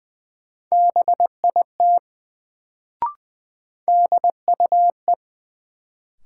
Morse code, heard as a keyed tone, about 20 wpm, 700 Hz. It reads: BIT DUE
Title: Sentences from Top 900 Words - New Word Drill - 20wpm